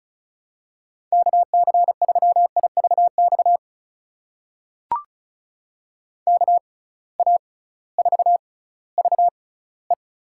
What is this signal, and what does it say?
Morse code, heard as a keyed tone, about 36 wpm, spaced as for 19 wpm, 700 Hz. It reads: KC3IVX KA4VE
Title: USA Call Signs - 6 Characters - 6x Character Spacing - 35wpm